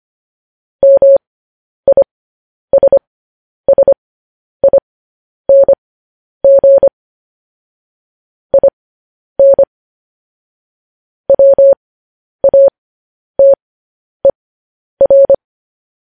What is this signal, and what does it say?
Morse code, heard as a keyed tone, about 25 wpm, spaced as for 10 wpm, 550 Hz. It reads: MISSING IN WATER
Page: news headlines in morse code at 10 wpm